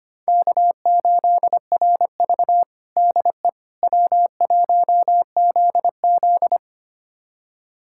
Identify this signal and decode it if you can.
Morse code, heard as a keyed tone, about 25 wpm, 700 Hz. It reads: K8RV DE W1ZZ
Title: USA Callsign DE Callsign - 4 Characters - 25wpm